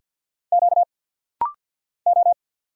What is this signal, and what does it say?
Morse code, heard as a keyed tone, about 50 wpm, 690 Hz. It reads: Q O
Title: Single Letters - Rapid-Fire 50wpm